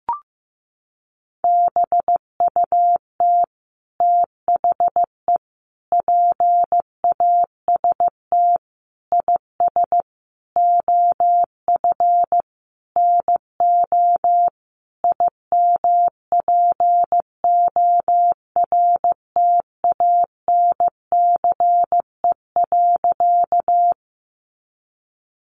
Morse code, heard as a keyed tone, about 15 wpm, 700 Hz. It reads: BUT THE PAST IS OF NO IMPORTANCE.
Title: Sentences from Top 1200 Words 15wpm